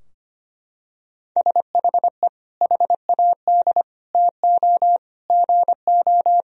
Morse code, heard as a keyed tone, about 25 wpm, 700 Hz. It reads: SHE HAD TO GO
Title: Sentences from Top 1000 Words - Review Part 2 - 25wpm